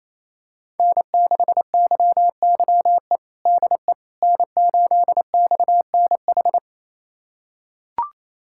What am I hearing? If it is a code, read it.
Morse code, heard as a keyed tone, about 28 wpm, 700 Hz. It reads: N6YYE DE N8XNH
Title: USA Callsign DE Callsign - 5 Characters - 28wpm